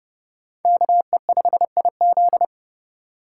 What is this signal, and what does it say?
Morse code, heard as a keyed tone, about 29 wpm, 700 Hz. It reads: KE5IZ